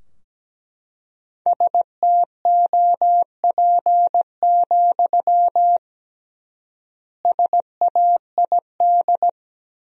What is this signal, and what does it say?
Morse code, heard as a keyed tone, about 17 wpm, 700 Hz. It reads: STOP, SAID
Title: 3x Word-Spacing - Sentences from Top 300 Words - 17wpm